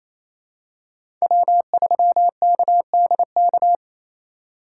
Morse code, heard as a keyed tone, about 28 wpm, 700 Hz. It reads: W3KDK
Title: USA Call Signs - 5 Characters - 4x Character Spacing - 28wpm